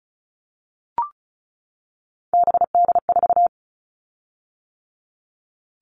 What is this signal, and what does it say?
Morse code, heard as a keyed tone, about 34 wpm, 700 Hz. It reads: BD4
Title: Callsign Prefixes for Top 100 DXCC - 35wpm